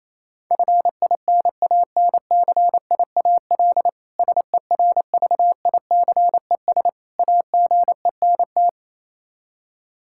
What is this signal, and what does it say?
Morse code, heard as a keyed tone, about 28 wpm, 700 Hz. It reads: FINANCIAL SERVICES AGENT
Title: Practice QSO - Exchange 3 - More Info - 28wpm